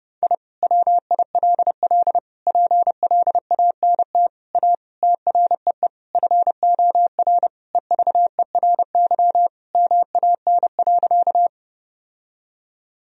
Morse code, heard as a keyed tone, about 30 wpm, 700 Hz. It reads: I WILL PLANT A TREE FOR EVERY MAN.
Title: Sentences from Top 1000 Words - Review Part 1 - 30wpm